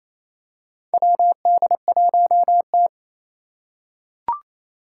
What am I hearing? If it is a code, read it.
Morse code, heard as a keyed tone, about 27 wpm, 700 Hz. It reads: WD1T
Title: Four Challenging Letter-Number Combinations 28wpm